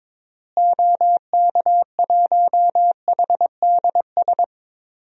Morse code, heard as a keyed tone, about 22 wpm, 700 Hz. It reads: OK1HDS
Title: DXCC-100 Call Signs - 6 Characters - Repeated 3x - 22wpm